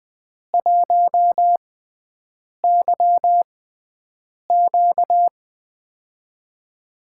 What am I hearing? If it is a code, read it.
Morse code, heard as a keyed tone, about 20 wpm, spaced as for 7 wpm, 700 Hz. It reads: 1YQ
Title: USA Call Signs - 4 Characters - 6x Character Spacing - 20wpm